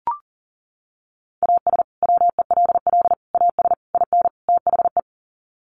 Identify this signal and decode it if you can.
Morse code, heard as a keyed tone, about 40 wpm, 700 Hz. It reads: AS WELL AS IN THE
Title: Sets of 5 Words Encore 40wpm